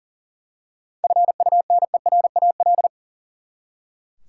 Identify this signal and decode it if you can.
Morse code, heard as a keyed tone, about 40 wpm, 700 Hz. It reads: FUNERAL
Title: Seven Letter Words - 40wpm